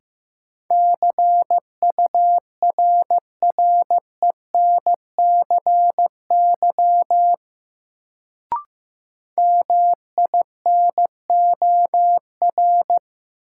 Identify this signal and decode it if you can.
Morse code, heard as a keyed tone, about 15 wpm, 700 Hz. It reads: CURRENCY MINOR
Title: Sentences from Top 1500 Words - New Word Drill Alt - 15wpm